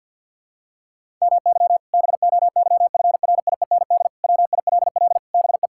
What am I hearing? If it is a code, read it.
Morse code, heard as a keyed tone, about 49 wpm, 700 Hz. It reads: MY BOYFRIEND WILL BE
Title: Sentences from Top 1000 Words - Review Part 5 - 50wpm